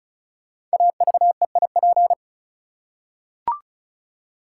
Morse code, heard as a keyed tone, about 36 wpm, 700 Hz. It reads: AVEIP
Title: Code Groups - 5 Letters - 35wpm